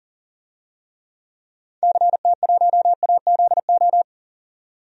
Morse code, heard as a keyed tone, about 40 wpm, 700 Hz. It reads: CT1AZO
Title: DXCC-100 Call Signs - 6 Characters - 40wpm